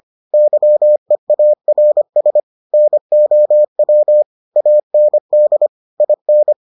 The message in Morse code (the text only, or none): YEARS NOW AND IN